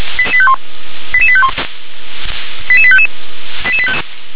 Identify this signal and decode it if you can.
Five-tone selective call (ZVEI), heard as quick runs of five tones, tones of about 70 ms. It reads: 88851 88855